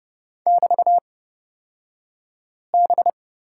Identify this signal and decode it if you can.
Morse code, heard as a keyed tone, about 30 wpm, spaced as for 5 wpm, 700 Hz. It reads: <BT>B
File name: Lesson 36d - BT - Two Characters 30wpm